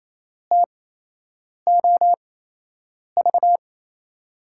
Morse code, heard as a keyed tone, about 27 wpm, spaced as for 8 wpm, 700 Hz. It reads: TOV
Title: USA Call Signs - 5 Characters - 8x Character Spacing - 28wpm